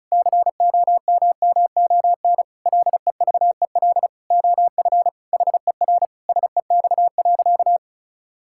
Morse code, heard as a keyed tone, about 34 wpm, 700 Hz. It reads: COMMON LEVEL OF HER SEX.
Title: Sentences from Top 1600 Words 35wpm